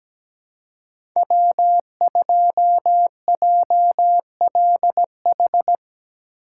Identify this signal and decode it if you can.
Morse code, heard as a keyed tone, about 17 wpm, 700 Hz. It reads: W2JLH
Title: USA Call Signs - 5 Characters - 3x Character Spacing - 17wpm